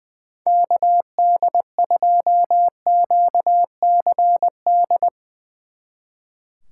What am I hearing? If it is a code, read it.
Morse code, heard as a keyed tone, about 20 wpm, 700 Hz. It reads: KD2QCD
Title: Speed Racing - USA 2x3 Call Signs 20wpm